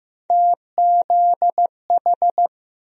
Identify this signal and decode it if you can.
Morse code, heard as a keyed tone, about 15 wpm, 700 Hz. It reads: TZH
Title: USA Call Sign Postfixes - 3 Letters 15wpm